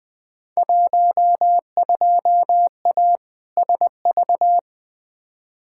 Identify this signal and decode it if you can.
Morse code, heard as a keyed tone, about 20 wpm, 700 Hz. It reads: 12A SV
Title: Field Day Exchanges Encore - 20wpm